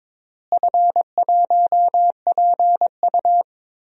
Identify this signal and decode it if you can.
Morse code, heard as a keyed tone, about 22 wpm, 700 Hz. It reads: F1PU
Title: DXCC-100 Call Signs - 4 Characters - Repeated 3x - 22wpm